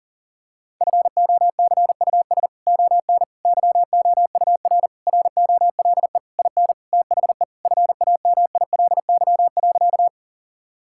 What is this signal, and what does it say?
Morse code, heard as a keyed tone, about 40 wpm, 700 Hz. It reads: FOCUS ON YOUR ROLE IN THE FAMILY.
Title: Sentences from Top 1000 Words - Review Part 6 - 40wpm